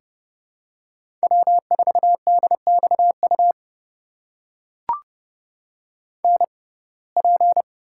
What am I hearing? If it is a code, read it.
Morse code, heard as a keyed tone, about 32 wpm, spaced as for 22 wpm, 700 Hz. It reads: W4DXU NP
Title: USA Call Signs - 5 Characters - 6x Character Spacing - 30wpm